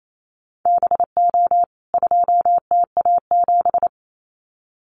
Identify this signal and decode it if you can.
Morse code, heard as a keyed tone, about 28 wpm, 700 Hz. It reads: BO 2TA7
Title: CWT Exchange 28wpm